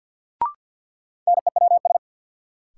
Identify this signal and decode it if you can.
Morse code, heard as a keyed tone, about 52 wpm, 700 Hz. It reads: NEWS
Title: Sets of 5 Words - New Word Drill - 50wpm